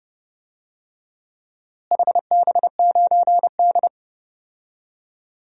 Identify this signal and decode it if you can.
Morse code, heard as a keyed tone, about 30 wpm, 700 Hz. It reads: HB9D
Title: DXCC-100 Call Signs - 4 Characters - Repeated 2x - 30wpm